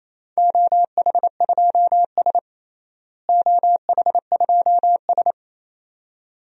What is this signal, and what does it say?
Morse code, heard as a keyed tone, about 28 wpm, 700 Hz. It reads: OH2S OH2S
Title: DXCC-100 Call Signs - 4 Characters - Repeated 2x - 28wpm